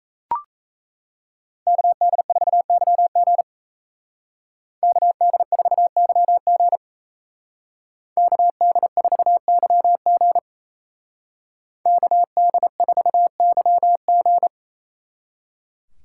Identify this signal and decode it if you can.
Morse code, heard as a keyed tone, about 36 wpm, 700 Hz. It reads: KD4YG KD4YG KD4YG KD4YG